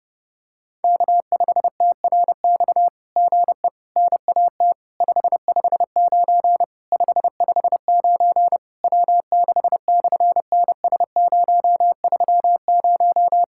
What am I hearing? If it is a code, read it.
Morse code, heard as a keyed tone, about 30 wpm, 700 Hz. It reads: K5TRX GE NAT 559 559 W6/NS030
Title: SOTA QSO Practice 30wpm